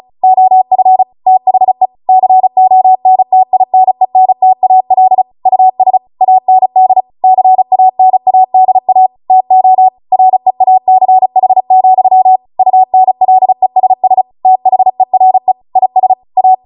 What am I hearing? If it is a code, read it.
Morse code, heard as a keyed tone, about 36 wpm, 750 Hz. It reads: OF THE CONTINENTAL US AND CANADA TO REACH, UNLESS THERE IS A